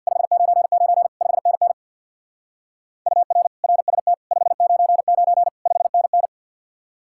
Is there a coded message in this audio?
Yes: Morse code from 0.1 to 6.3 s, heard as a keyed tone, about 65 wpm, 700 Hz. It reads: 599 5NN UR RST 599 5NN